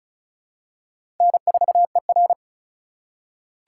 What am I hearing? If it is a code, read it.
Morse code, heard as a keyed tone, about 34 wpm, 700 Hz. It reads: N4ER
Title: USA Call Signs - 4 Characters - 2x Character Spacing - 35wpm